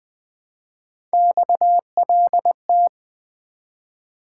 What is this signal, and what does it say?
Morse code, heard as a keyed tone, about 20 wpm, 700 Hz. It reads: XLT
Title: Three Letter-Number Combinations - 20wpm